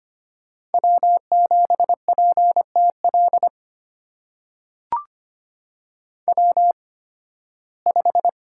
Morse code, heard as a keyed tone, about 25 wpm, spaced as for 16 wpm, 700 Hz. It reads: W7PTL W5
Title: USA Call Signs - 8x Character Spacing - 25wpm